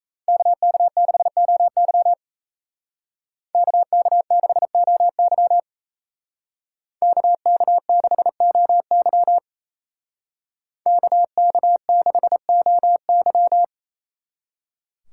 Morse code, heard as a keyed tone, about 33 wpm, 700 Hz. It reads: KK6OY KK6OY KK6OY KK6OY
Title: Speed Racing - USA 2x2 Call Signs 28wpm